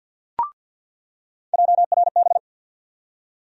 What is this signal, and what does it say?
Morse code, heard as a keyed tone, about 48 wpm, 700 Hz. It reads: JRB
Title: USA Call Sign Postfixes - 3 Letters 50wpm